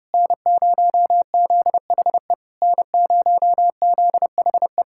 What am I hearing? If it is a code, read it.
Morse code, heard as a keyed tone, about 33 wpm, 700 Hz. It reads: N0ZHE N0ZHE